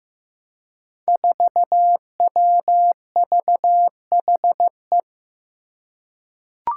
Morse code, heard as a keyed tone, about 16 wpm, 700 Hz. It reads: 4WVHE
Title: Five Challenging Letter-Number Combinations 15wpm